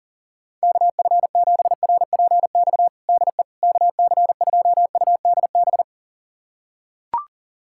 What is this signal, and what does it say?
Morse code, heard as a keyed tone, about 40 wpm, 700 Hz. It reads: KF7RPX DE KC2UDB
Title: USA Callsign DE Callsign - 6 Characters - 40wpm